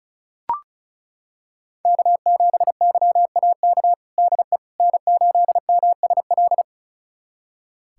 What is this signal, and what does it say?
Morse code, heard as a keyed tone, about 34 wpm, 700 Hz. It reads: K7YAK DE N8MSL
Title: USA Callsign DE Callsign - 5 Characters - 35wpm